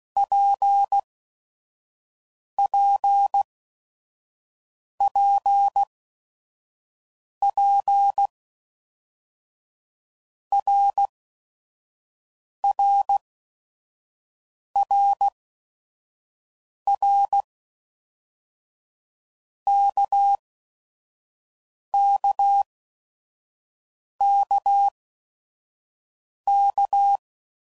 Morse code, heard as a keyed tone, about 16 wpm, spaced as for 5 wpm, 800 Hz. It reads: PPPP RRRR KKKK
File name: Basic Letter Groups By7Groups / RKPX / RKPX@5 f15